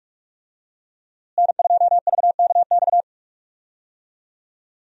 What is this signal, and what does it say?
Morse code, heard as a keyed tone, about 44 wpm, 700 Hz. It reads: N2VKX
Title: USA Call Signs - 5 Characters - Repeated 2x - 45wpm